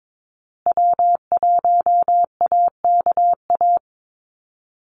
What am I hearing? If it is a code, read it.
Morse code, heard as a keyed tone, about 22 wpm, 700 Hz. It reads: W1AKA
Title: USA Call Signs Repeated 3x 22wpm